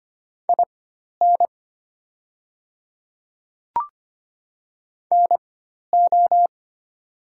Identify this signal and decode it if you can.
Morse code, heard as a keyed tone, about 25 wpm, spaced as for 12 wpm, 700 Hz. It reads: IN NO
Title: 4x Character Spacing - Top 100 Words - 25wpm